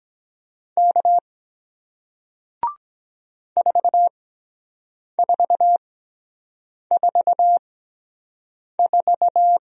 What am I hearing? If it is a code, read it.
Morse code, heard as a keyed tone, about 23 wpm, spaced as for 7 wpm, 700 Hz. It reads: K 4444